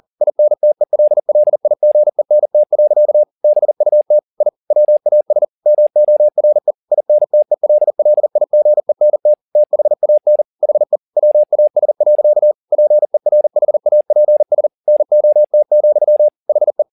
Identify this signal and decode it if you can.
Morse code, heard as a keyed tone, about 40 wpm, 600 Hz. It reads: INTELLIGENT. BUT I WAS MORE INTELLIGENT THAN HE WAS. PERHAPS NOT, HE
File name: the old man and the sea 2 40wpm